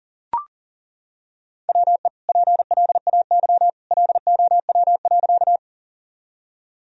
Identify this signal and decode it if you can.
Morse code, heard as a keyed tone, about 40 wpm, 700 Hz. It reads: WE PLAY LOW.
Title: Sentences from Top 1000 Words - Review Part 1 - 40wpm